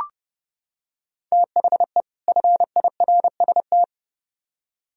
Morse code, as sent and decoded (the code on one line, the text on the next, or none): - .... / ..-. .. .-. ... -
THE FIRST